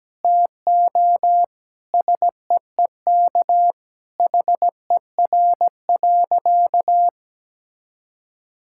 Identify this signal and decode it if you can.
Morse code, heard as a keyed tone, about 17 wpm, 700 Hz. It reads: TO SEEK HER.
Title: Sentences from Top 1600 Words 17wpm